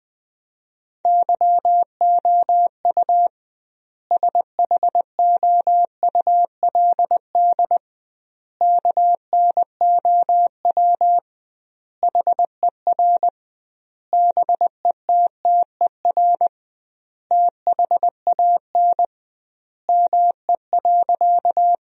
Morse code, heard as a keyed tone, about 20 wpm, 700 Hz. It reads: YOU SHOULD KNOW HER BETTER THAN ME.
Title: 2x Word-Spacing - Sentences from Top 400 Words - 20wpm